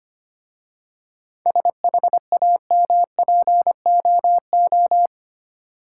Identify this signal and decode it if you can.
Morse code, heard as a keyed tone, about 25 wpm, 690 Hz. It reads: SHAMPOO
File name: practice / 25wpm / Easy Words with Hints - 25wpm